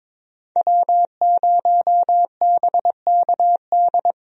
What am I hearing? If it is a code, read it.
Morse code, heard as a keyed tone, about 22 wpm, 700 Hz. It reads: W0BKD
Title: USA Call Signs - 5 Characters - Repeated 2x - 22wpm